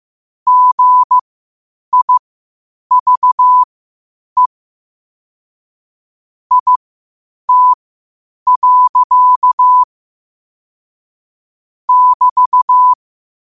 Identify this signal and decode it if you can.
Morse code, heard as a keyed tone, about 15 wpm, spaced as for 8 wpm, 1000 Hz. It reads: GIVE IT. <BT>